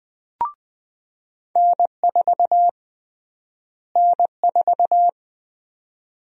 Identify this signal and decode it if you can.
Morse code, heard as a keyed tone, about 20 wpm, 700 Hz. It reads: N4 N4
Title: USA Call Sign Prefixes - Repeated 2x - 20wpm